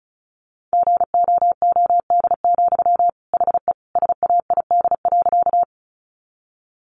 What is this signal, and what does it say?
Morse code, heard as a keyed tone, about 34 wpm, 700 Hz. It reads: GOOD, HE SAID.
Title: Sentences from Top 200 Words 35wpm